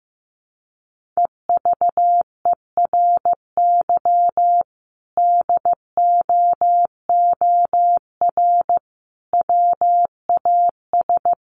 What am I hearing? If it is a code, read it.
Morse code, heard as a keyed tone, about 15 wpm, 700 Hz. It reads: EVERY DOOR WAS
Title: Sentences from Top 1400 Words 15wpm